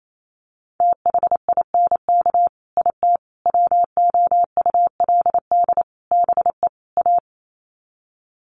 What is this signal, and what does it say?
Morse code, heard as a keyed tone, about 28 wpm, 700 Hz. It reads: THINK IT WOULD BE A
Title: Sets of 5 Words Encore2 28wpm